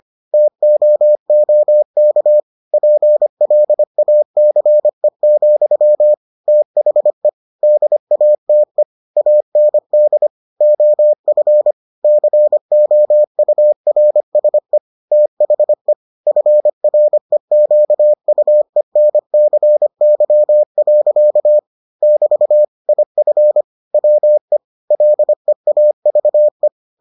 Morse code, heard as a keyed tone, about 25 wpm, 600 Hz. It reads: TOOK PLACE, THE DATE AND OF COURSE THE FREQUENCY. <BT> IF WE LEAVE